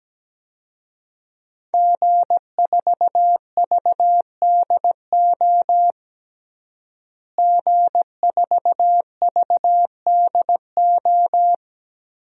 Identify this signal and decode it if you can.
Morse code, heard as a keyed tone, about 17 wpm, 700 Hz. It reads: G4VDO G4VDO
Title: DXCC-100 Call Signs - 5 Characters - Repeated 3x - 17wpm